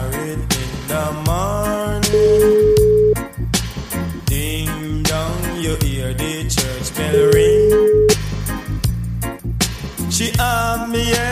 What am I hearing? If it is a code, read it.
Ringback tone (Europe, 425 Hz), with noise.